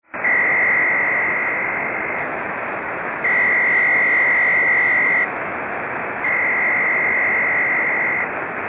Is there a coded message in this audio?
Yes: Emergency Alert System (SAME) tones (header bursts) from 0.1 to 8.3 s, heard as harsh buzzing bursts.